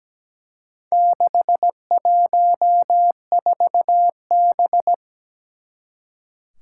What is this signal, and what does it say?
Morse code, heard as a keyed tone, about 17 wpm, 700 Hz. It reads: 614B
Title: Four Challenging Letter-Number Combinations 17wpm